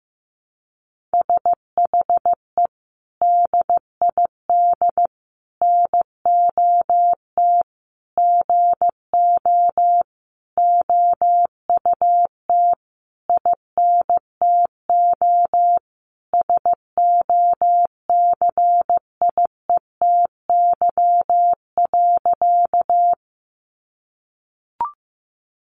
Morse code, heard as a keyed tone, about 15 wpm, 700 Hz. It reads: SHE DID NOT GO OUT INTO SOCIETY.
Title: Sentences from Top 700 Words 15wpm